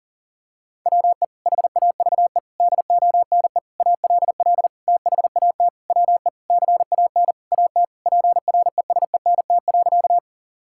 Morse code, heard as a keyed tone, about 40 wpm, 700 Hz. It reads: WE HAVE DONE ALL THAT WE CAN AT PRESENT.
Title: Sentences from Top 1000 Words - Review Part 5 - 40wpm